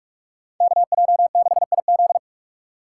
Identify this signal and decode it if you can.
Morse code, heard as a keyed tone, about 47 wpm, 700 Hz. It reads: KJ6IZ
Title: USA Call Signs - 2x Character Spacing - 45wpm